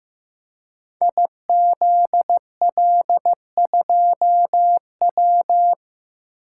Morse code, heard as a keyed tone, about 15 wpm, 700 Hz. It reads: IZL2W
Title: Code Groups - 5 Letter:Numbers - 15wpm